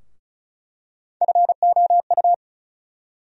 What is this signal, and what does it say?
Morse code, heard as a keyed tone, about 34 wpm, 700 Hz. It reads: FOU
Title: USA Call Sign Postfixes - 3 Letters 35wpm